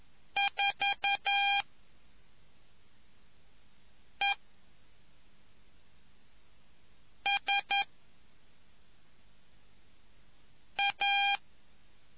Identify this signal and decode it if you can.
Morse code, heard as a keyed tone, about 11 wpm, 2400 Hz. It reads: 4 E S A